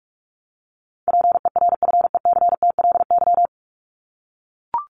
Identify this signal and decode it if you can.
Morse code, heard as a keyed tone, about 44 wpm, 700 Hz. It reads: PERFECTLY
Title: LY Suffix - Top 100 Words with Base Words 45wpm